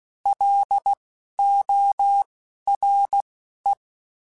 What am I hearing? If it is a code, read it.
Morse code, heard as a keyed tone, about 16 wpm, 800 Hz. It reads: L O R E
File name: Alphabetical / Alphabetical mp3s / Ls @10